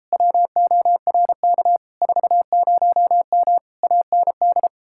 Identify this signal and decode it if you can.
Morse code, heard as a keyed tone, about 38 wpm, 700 Hz. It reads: WORK 40M AND